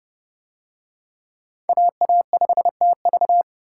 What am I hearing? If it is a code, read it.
Morse code, heard as a keyed tone, about 29 wpm, 700 Hz. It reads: AA5TV